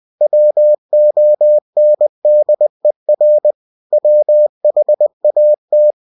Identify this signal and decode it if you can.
Morse code, heard as a keyed tone, about 20 wpm, 600 Hz. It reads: WONDER WHAT
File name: cw-books / winnie-the-pooh / winnie-20wpm-page-0092